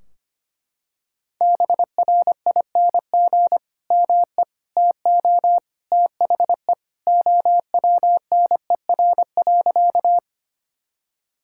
Morse code, heard as a keyed tone, about 25 wpm, 700 Hz. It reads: BRING ME TO THE OWNER.
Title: Sentences from Top 1600 Words 25wpm